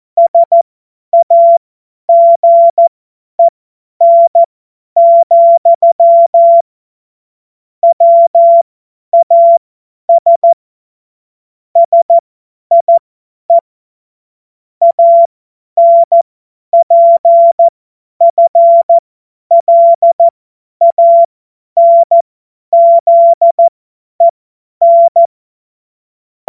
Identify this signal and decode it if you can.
Morse code, heard as a keyed tone, about 14 wpm, spaced as for 10 wpm, 680 Hz. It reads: SAGEN, WAS SIE ANPFLANZEN
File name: Morsekurs / Witze / Witz 06. Juli 2021, 50 F70